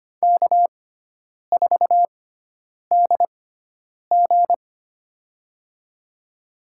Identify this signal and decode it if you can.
Morse code, heard as a keyed tone, about 25 wpm, spaced as for 9 wpm, 700 Hz. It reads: K4DG
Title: USA Call Signs - 6x Character Spacing - 25wpm